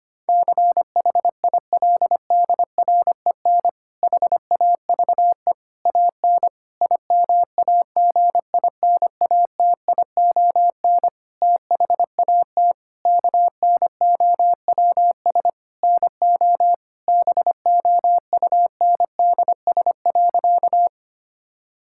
Morse code, heard as a keyed tone, about 25 wpm, 700 Hz. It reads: CHILDREN HAVE AN IMAGINATION THAT KNOWS NO BOUNDS.
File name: practice / 25wpm / Difficult Sentences with Hints - 25wpm